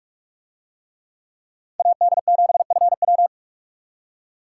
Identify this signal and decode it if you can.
Morse code, heard as a keyed tone, about 44 wpm, 700 Hz. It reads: AD7FW